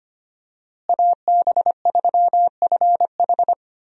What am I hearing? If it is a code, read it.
Morse code, heard as a keyed tone, about 25 wpm, 700 Hz. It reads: AB3FH
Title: USA Call Signs - 5 Characters - Repeated 2x - 25wpm